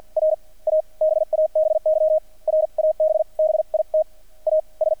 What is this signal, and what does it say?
Morse code, heard as a keyed tone, about 50 wpm, spaced as for 39 wpm, 630 Hz. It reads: R A BABY RABBIT AS
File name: archived / CRSnet